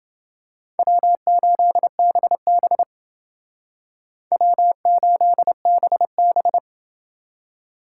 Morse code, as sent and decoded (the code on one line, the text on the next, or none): .-- ---.. -... -... / .-- ---.. -... -...
W8BB W8BB